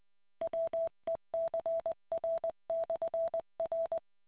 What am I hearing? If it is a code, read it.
Morse code, heard as a keyed tone, about 20 wpm, 660 Hz. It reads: WECR/R